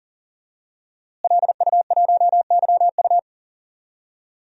Morse code, heard as a keyed tone, about 37 wpm, 700 Hz. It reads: LU1YU
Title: DXCC-100 Call Signs - 5 Characters - Repeated 2x - 40wpm